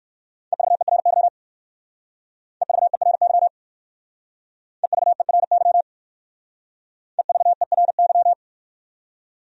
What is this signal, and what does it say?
Morse code, heard as a keyed tone, about 55 wpm, 700 Hz. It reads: EVERY EVERY EVERY EVERY
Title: Speed-Racing Top 300 Words 45wpm